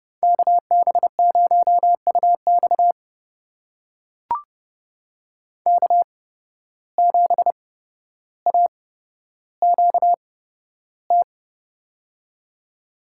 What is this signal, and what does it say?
Morse code, heard as a keyed tone, about 30 wpm, spaced as for 13 wpm, 700 Hz. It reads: KB0UX K7AQT
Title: USA Call Signs - 5 Characters - 8x Character Spacing - 30wpm